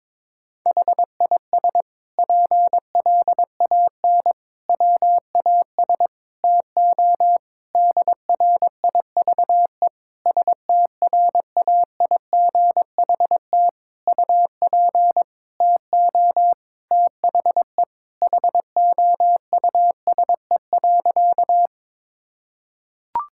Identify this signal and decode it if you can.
Morse code, heard as a keyed tone, about 22 wpm, 700 Hz. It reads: HIS PLAN WAS TO DRIVE STRAIGHT UP TO THE HOUSE.
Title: Sentences from Top 1000 Words - Review Part 6 - 22wpm